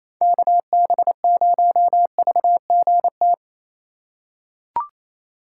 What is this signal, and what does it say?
Morse code, heard as a keyed tone, about 28 wpm, 690 Hz. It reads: KB0VGT
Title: USA Call Signs - 6 Characters - 2x Character Spacing - 28wpm